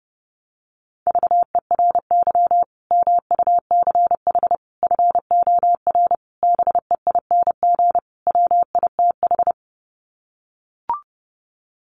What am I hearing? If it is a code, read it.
Morse code, heard as a keyed tone, about 30 wpm, 700 Hz. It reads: VERY MUCH FOR BEING WITH
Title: Sets of 5 Words 30wpm